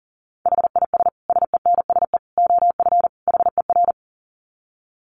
Morse code, heard as a keyed tone, about 40 wpm, 700 Hz. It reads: HIS SENSE OF HER
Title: Dit Challenge 40wpm